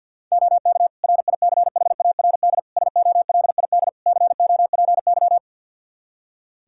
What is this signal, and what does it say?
Morse code, heard as a keyed tone, about 51 wpm, 700 Hz. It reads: OK RICHARD SOLID COPY
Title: Practice QSO - Exchange 3 - More Info - 50wpm